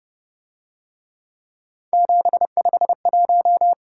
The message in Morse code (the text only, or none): 751